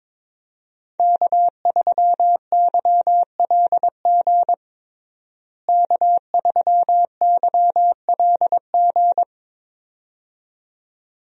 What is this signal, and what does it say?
Morse code, heard as a keyed tone, about 22 wpm, 700 Hz. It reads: K3YLG K3YLG
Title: USA Call Signs - 5 Characters - Repeated 2x - 22wpm